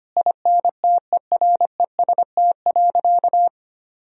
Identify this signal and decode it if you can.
Morse code, heard as a keyed tone, about 25 wpm, 700 Hz. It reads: INTEREST.
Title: Sentences from Top 1000 Words - Review Part 2 - 25wpm